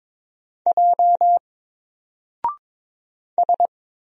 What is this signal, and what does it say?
Morse code, heard as a keyed tone, about 22 wpm, 700 Hz. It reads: J S
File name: Single-Letter-Number-22wpm